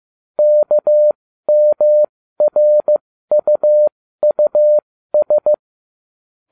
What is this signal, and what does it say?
Morse code, heard as a keyed tone, about 15 wpm, spaced as for 13 wpm, 600 Hz. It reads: KMRUUS